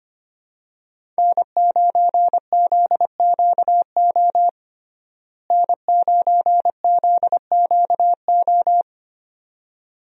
Morse code, heard as a keyed tone, about 25 wpm, 700 Hz. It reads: N9ZQO N9ZQO
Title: USA Call Signs - 5 Characters - Repeated 2x - 25wpm